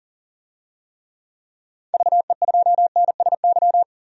Morse code, heard as a keyed tone, about 38 wpm, 700 Hz. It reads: VE2NSY
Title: DXCC-100 Call Signs - 6 Characters - 40wpm